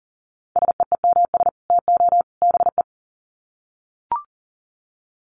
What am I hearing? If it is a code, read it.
Morse code, heard as a keyed tone, about 38 wpm, 700 Hz. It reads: SEEMS TO BE
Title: Sets of 3 Words 40wpm